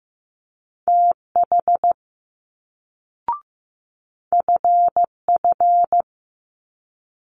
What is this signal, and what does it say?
Morse code, heard as a keyed tone, about 15 wpm, 700 Hz. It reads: TH FF
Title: Multi-letter Phonemes Quick Fire 15wpm